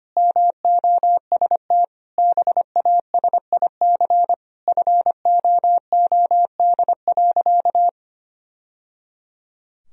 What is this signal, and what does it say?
Morse code, heard as a keyed tone, about 25 wpm, 700 Hz. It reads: MOST BASIC FOOD.
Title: Sentences from Top 1000 Words - Review Part 3 - 25wpm